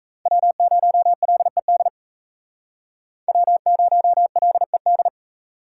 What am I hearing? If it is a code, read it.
Morse code, heard as a keyed tone, about 35 wpm, 700 Hz. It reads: W0LED W0LED